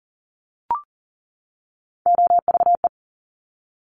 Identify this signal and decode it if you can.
Morse code, heard as a keyed tone, about 42 wpm, 700 Hz. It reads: OVE
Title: Top Three Letter Combinations in Words 40wpm